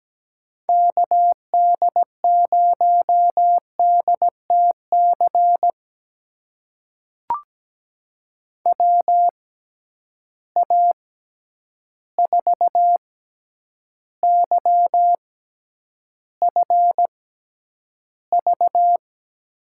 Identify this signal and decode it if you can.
Morse code, heard as a keyed tone, about 17 wpm, spaced as for 9 wpm, 700 Hz. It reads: KD0DTC WA4YFV
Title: USA Call Signs - 6 Characters - 6x Character Spacing - 17wpm